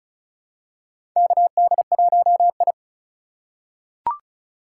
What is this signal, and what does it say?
Morse code, heard as a keyed tone, about 36 wpm, 700 Hz. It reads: KD1I
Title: USA Call Signs - 3x Character Spacing - 35wpm